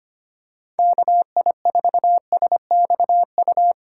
Morse code, heard as a keyed tone, about 25 wpm, 700 Hz. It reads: KI4SXU
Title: USA Call Signs - 6 Characters - 3x Character Spacing - 25wpm